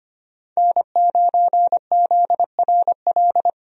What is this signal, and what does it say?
Morse code, heard as a keyed tone, about 25 wpm, 700 Hz. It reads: N9ZRL
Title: USA Call Signs - 5 Characters - 25wpm